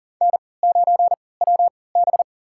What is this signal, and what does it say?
Morse code, heard as a keyed tone, about 40 wpm, spaced as for 23 wpm, 700 Hz. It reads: N9WB
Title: USA Call Signs - 4 Characters - 3x Character Spacing - 40wpm